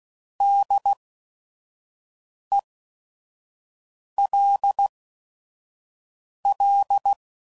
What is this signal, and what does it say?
Morse code, heard as a keyed tone, about 16 wpm, spaced as for 5 wpm, 800 Hz. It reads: DELL
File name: Prefixes / By First Letter / Ds